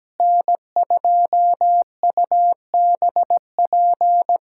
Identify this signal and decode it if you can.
Morse code, heard as a keyed tone, about 17 wpm, 700 Hz. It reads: N2UBP
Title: USA Callsign DE Callsign - 5 Characters - 17wpm